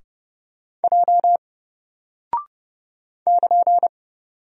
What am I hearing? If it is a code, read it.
Morse code, heard as a keyed tone, about 30 wpm, 700 Hz. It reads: J <KN>
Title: Single Character (Letter:Number:Punctuation:Prosign) Alt 30wpm